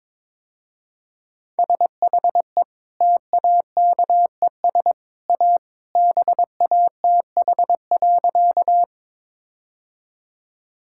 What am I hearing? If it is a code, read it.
Morse code, heard as a keyed tone, about 22 wpm, 700 Hz. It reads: SHE TAKES A BATH.
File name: practice / 22wpm / Easy Sentences with Hints - 22wpm